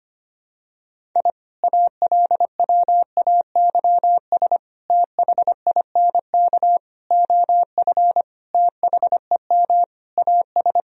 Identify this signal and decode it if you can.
Morse code, heard as a keyed tone, about 25 wpm, 700 Hz. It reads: I ALWAYS THINK OF THEM AS
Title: Sentences from Top 1000 Words - Review Part 3 - 25wpm